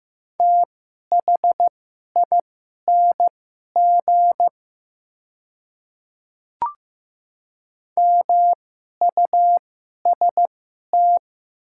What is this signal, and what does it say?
Morse code, heard as a keyed tone, about 15 wpm, spaced as for 11 wpm, 700 Hz. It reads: THING MUST
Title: 2x Character Spacing - Top 200 Words - 15wpm